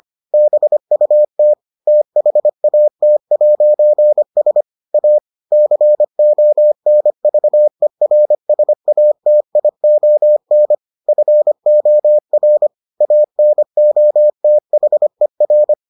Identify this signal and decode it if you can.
Morse code, heard as a keyed tone, about 25 wpm, 600 Hz. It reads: BUT THAT'S A CONVERSATION FOR ANOTHER